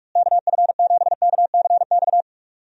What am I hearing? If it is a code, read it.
Morse code, heard as a keyed tone, about 46 wpm, 700 Hz. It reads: KF7KCX
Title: USA Call Signs - 6 Characters - Repeated 3x - 45wpm